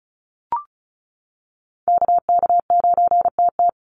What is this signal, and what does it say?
Morse code, heard as a keyed tone, about 35 wpm, 700 Hz. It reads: KK9TT
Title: USA Call Signs 35wpm